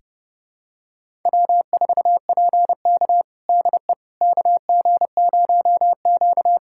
Morse code, heard as a keyed tone, about 30 wpm, 700 Hz. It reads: W4PK DE KG0Q
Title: USA Callsign DE Callsign - 4 Characters - 30wpm